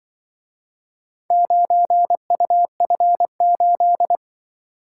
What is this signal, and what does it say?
Morse code, heard as a keyed tone, about 25 wpm, 700 Hz. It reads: 9UF8
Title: Speed-Racing - Four Letter-Number Combinations - 20wpm